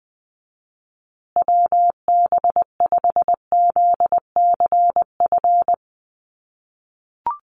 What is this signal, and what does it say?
Morse code, heard as a keyed tone, about 20 wpm, 700 Hz. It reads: WB5ZCF
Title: USA Call Signs Repeated 3x 20wpm